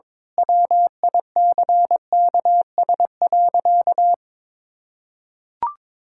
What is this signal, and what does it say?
Morse code, heard as a keyed tone, about 22 wpm, 700 Hz. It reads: WICKS.
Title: Pangram Challenge v2 - 22wpm